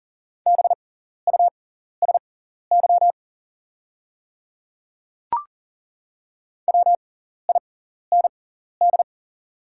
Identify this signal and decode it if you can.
Morse code, heard as a keyed tone, about 40 wpm, spaced as for 14 wpm, 700 Hz. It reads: BUSY WIND
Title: 6x Character Spacing - Top 500 Words - 40wpm